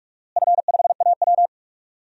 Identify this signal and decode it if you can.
Morse code, heard as a keyed tone, about 46 wpm, 700 Hz. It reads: F5AW
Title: DXCC-100 Call Signs - 4 Characters - Repeated 3x - 45wpm